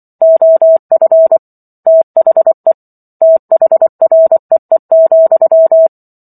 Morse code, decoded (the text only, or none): OF THE THREE,